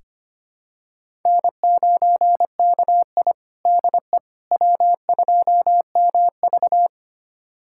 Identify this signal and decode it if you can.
Morse code, heard as a keyed tone, about 25 wpm, 700 Hz. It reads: N9KI DE W2MV
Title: USA Callsign DE Callsign - 4 Characters - 25wpm